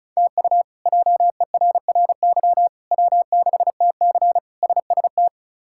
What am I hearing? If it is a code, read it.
Morse code, heard as a keyed tone, about 36 wpm, 700 Hz. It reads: TU JERRY W6TC SST
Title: K1USN SST QSO Practice - 35wpm